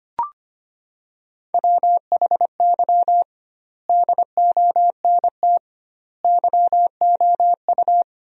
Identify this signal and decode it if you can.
Morse code, heard as a keyed tone, about 25 wpm, 700 Hz. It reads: WHY DONT YOU
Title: 2x Word-Spacing - Sentences from Top 300 Words - 25wpm